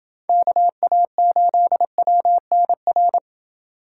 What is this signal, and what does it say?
Morse code, heard as a keyed tone, about 27 wpm, 700 Hz. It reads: KA8WNR